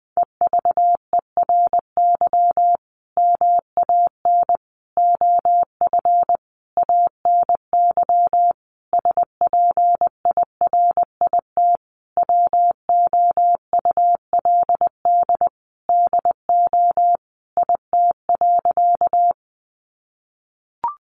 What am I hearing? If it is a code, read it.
Morse code, heard as a keyed tone, about 19 wpm, 700 Hz. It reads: EVERY MAN OF ANY SPIRIT WOULD DO IT.
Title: Sentences from Top 1400 Words 20wpm